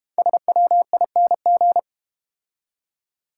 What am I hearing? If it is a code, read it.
Morse code, heard as a keyed tone, about 30 wpm, 700 Hz. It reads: SWING
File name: Easy Words - Speed-Racing with Hints - 25wpm